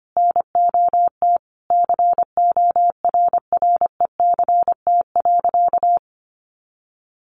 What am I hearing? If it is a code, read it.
Morse code, heard as a keyed tone, about 25 wpm, 700 Hz. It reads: NOT CORRECT.
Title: Sentences from Top 1200 Words 25wpm